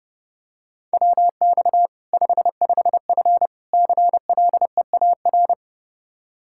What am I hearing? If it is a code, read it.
Morse code, heard as a keyed tone, about 32 wpm, 700 Hz. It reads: WX 55F CLEAR
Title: Practice QSO - Exchange 2 - Info - 30wpm